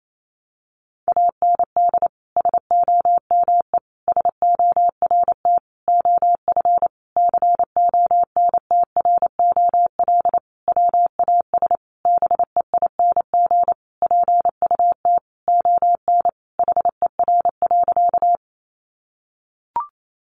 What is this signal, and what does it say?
Morse code, heard as a keyed tone, about 28 wpm, 700 Hz. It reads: AND SOME SORT OF CONTROL WAS BEING PUT ON HER.
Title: Sentences from Top 1300 Words 28wpm